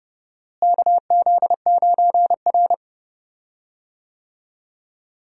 Morse code, heard as a keyed tone, about 32 wpm, 700 Hz. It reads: KZ9R
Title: USA Call Signs - 4 Characters - Repeated 3x - 30wpm